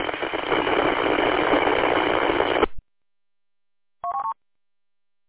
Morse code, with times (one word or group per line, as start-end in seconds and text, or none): none